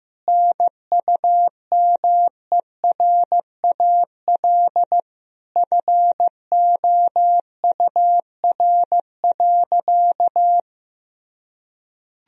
Morse code, heard as a keyed tone, about 15 wpm, 700 Hz. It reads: NUMERAL FOUR.
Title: Sentences from Top 1000 Words - Review Part 2 - 15wpm